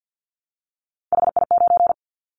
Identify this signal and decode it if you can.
Morse code, heard as a keyed tone, about 50 wpm, 700 Hz. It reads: HI9
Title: Callsign Prefixes for Top 100 DXCC - 50wpm